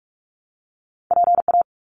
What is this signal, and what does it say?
Morse code, heard as a keyed tone, about 44 wpm, 700 Hz. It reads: PA
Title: US States Repeated 3x 45wpm